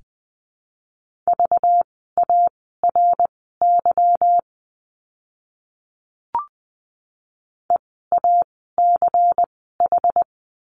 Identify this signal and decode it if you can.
Morse code, heard as a keyed tone, about 20 wpm, spaced as for 14 wpm, 700 Hz. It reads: VARY EACH
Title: 192 Lesson 31f - Letter v - Words 20wpm